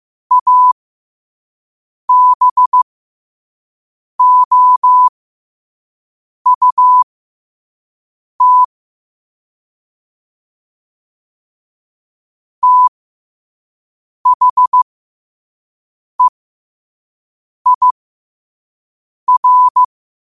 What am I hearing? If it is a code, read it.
Morse code, heard as a keyed tone, about 15 wpm, spaced as for 5 wpm, 1000 Hz. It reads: ABOUT THEIR
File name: qotdfiles / 5wpm